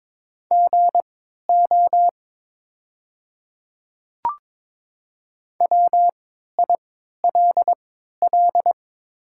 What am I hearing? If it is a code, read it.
Morse code, heard as a keyed tone, about 22 wpm, spaced as for 13 wpm, 700 Hz. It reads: GO WILL
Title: 3x Character Spacing - Top 200 Words - 22wpm